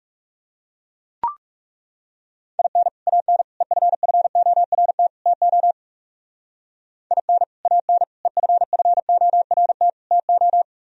Morse code, heard as a keyed tone, about 36 wpm, 700 Hz. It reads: IN AN EFFORT TO IN AN EFFORT TO
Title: Speed-Racing - Sets of 4 Words 30wpm